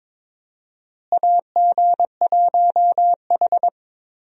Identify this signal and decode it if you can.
Morse code, heard as a keyed tone, about 22 wpm, 700 Hz. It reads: AG1H
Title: USA Call Signs - 4 Characters - Repeated 2x - 22wpm